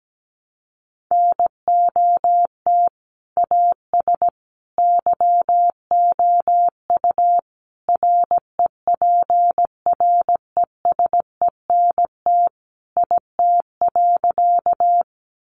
Morse code, heard as a keyed tone, about 17 wpm, 700 Hz. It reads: NOT AS YOU REPRESENT IT.
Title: Sentences from Top 1200 Words 17wpm